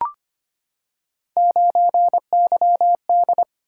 Morse code, heard as a keyed tone, about 25 wpm, 700 Hz. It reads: 9YD
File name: Three Letter-Number Combinations - 25wpm